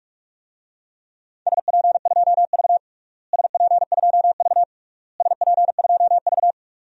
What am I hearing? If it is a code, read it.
Morse code, heard as a keyed tone, about 46 wpm, 690 Hz. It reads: SP2V SP2V SP2V
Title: DXCC-100 Call Signs - 4 Characters - Repeated 3x - 45wpm